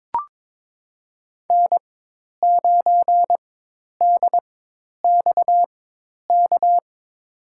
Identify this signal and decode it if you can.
Morse code, heard as a keyed tone, about 22 wpm, spaced as for 10 wpm, 700 Hz. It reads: N9DXK
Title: USA Call Signs - 5 Characters - 4x Character Spacing - 22wpm